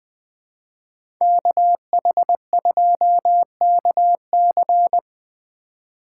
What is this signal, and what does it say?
Morse code, heard as a keyed tone, about 20 wpm, 700 Hz. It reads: KH2KC
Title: USA Call Signs - 5 Characters - 3x Character Spacing - 20wpm